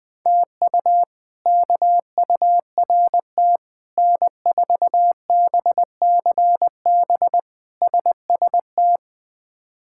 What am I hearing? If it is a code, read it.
Morse code, heard as a keyed tone, about 20 wpm, 700 Hz. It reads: TU KURT N4BCB SST
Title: K1USN SST QSO Practice - 20wpm